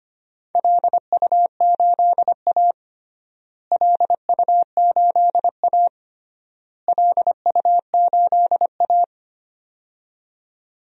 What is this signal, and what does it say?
Morse code, heard as a keyed tone, about 25 wpm, 700 Hz. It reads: LU8A LU8A LU8A
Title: DXCC-100 Call Signs - 4 Characters - Repeated 3x - 25wpm